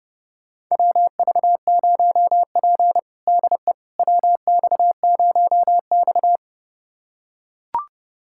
Morse code, heard as a keyed tone, about 30 wpm, 700 Hz. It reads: WV0P DE WX0X
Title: USA Callsign DE Callsign - 4 Characters - 30wpm